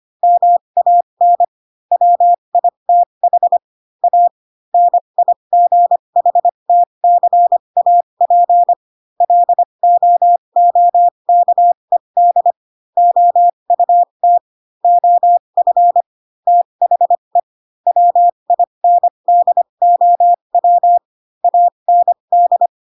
Morse code, heard as a keyed tone, about 25 wpm, spaced as for 22 wpm, 700 Hz. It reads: MAN WITH A NIGHTCAP LOOKED OUT OF THE WINDOW AND
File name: Pinocchio / Pinocchio ch7 18 25 ik1hge